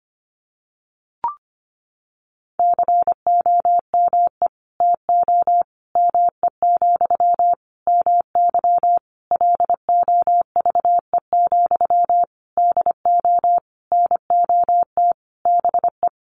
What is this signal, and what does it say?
Morse code, heard as a keyed tone, about 25 wpm, 700 Hz. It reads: COME TO ME, MY LOVE, DO NOT BE
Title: Sentences from Top 1500 Words 25wpm